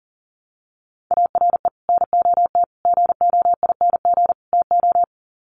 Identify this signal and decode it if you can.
Morse code, heard as a keyed tone, about 37 wpm, 700 Hz. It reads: ARE NOT GOING TO
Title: Sets of 4 Words 40wpm